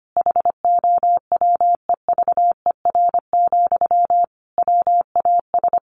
Morse code, heard as a keyed tone, about 25 wpm, 700 Hz. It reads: HOWEVER, WAS
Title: Sentences from Top 1400 Words 25wpm